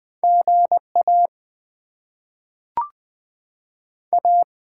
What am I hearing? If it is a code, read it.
Morse code, heard as a keyed tone, about 20 wpm, 700 Hz. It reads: GA A
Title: Top 20 QSO Elements - 6x Character Spacing - 20wpm